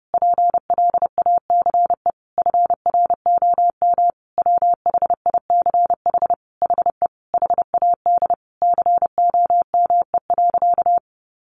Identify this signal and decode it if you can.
Morse code, heard as a keyed tone, about 30 wpm, 700 Hz. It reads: PLACE FROM WHICH HE HAD COME.
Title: Sentences from Top 1500 Words 30wpm